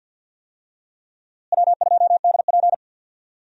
Morse code, heard as a keyed tone, about 51 wpm, 700 Hz. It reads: W2DP
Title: USA Call Signs - 4 Characters - Repeated 3x - 50wpm